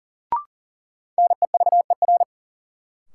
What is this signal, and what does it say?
Morse code, heard as a keyed tone, about 40 wpm, 700 Hz. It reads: NEVER
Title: Sentences from Top 300 Words - New Word Drill - 40wpm